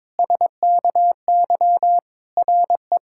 Morse code, heard as a keyed tone, about 22 wpm, 700 Hz. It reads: SKY RE
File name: Difficult Sentences with Hints - 22wpm